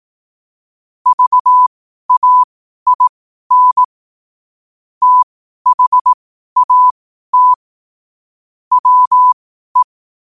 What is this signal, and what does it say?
Morse code, heard as a keyed tone, about 18 wpm, spaced as for 12 wpm, 1000 Hz. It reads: VAIN THAT WE
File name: qotdfiles / 12wpm